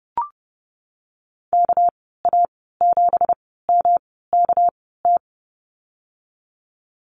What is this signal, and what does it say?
Morse code, heard as a keyed tone, about 30 wpm, spaced as for 17 wpm, 700 Hz. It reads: KA7MKT
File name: Lesson 37g - Letter x - Call Signs 30wpm